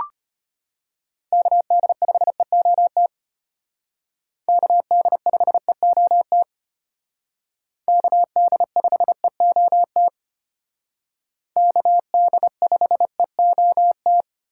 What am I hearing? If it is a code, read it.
Morse code, heard as a keyed tone, about 32 wpm, 700 Hz. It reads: KD5EOT KD5EOT KD5EOT KD5EOT